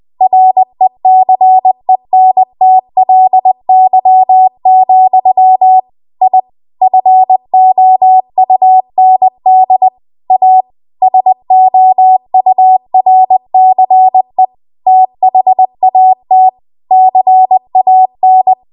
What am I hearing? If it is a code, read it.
Morse code, heard as a keyed tone, about 20 wpm, 750 Hz. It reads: RECENTLY, I FOUND A SOURCE THAT CAN
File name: Archive / 20 WPM / 20WPM